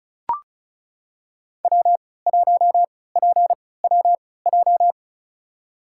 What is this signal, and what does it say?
Morse code, heard as a keyed tone, about 36 wpm, spaced as for 20 wpm, 700 Hz. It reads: W1PWJ
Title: USA Call Signs - 5 Characters - 3x Character Spacing - 35wpm